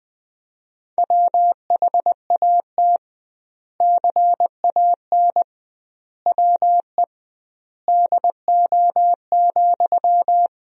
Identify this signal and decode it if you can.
Morse code, heard as a keyed tone, about 20 wpm, 700 Hz. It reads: WHAT CAN WE DO,
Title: 2x Word-Spacing - Sentences from Top 100 Words - 20wpm